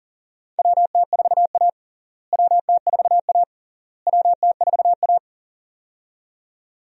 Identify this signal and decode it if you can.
Morse code, heard as a keyed tone, about 40 wpm, 700 Hz. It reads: WT4A WT4A WT4A
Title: USA Call Signs - 4 Characters - Repeated 3x - 40wpm